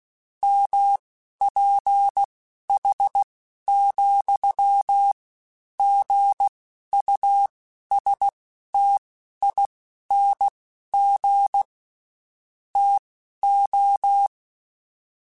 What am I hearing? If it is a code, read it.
Morse code, heard as a keyed tone, about 16 wpm, spaced as for 11 wpm, 800 Hz. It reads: MPH,GUSTING TO
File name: Spaced Characters / Spaced Characters-mp3s / Weather4 21 sp @15